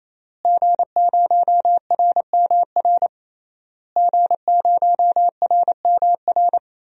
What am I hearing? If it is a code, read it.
Morse code, heard as a keyed tone, about 28 wpm, 700 Hz. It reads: G0RMR G0RMR
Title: DXCC-100 Call Signs - 5 Characters - Repeated 3x - 28wpm